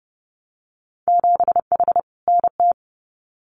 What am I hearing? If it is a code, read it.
Morse code, heard as a keyed tone, about 30 wpm, 700 Hz. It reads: 7H NT
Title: Winter Field Day Exchanges Encore 30wpm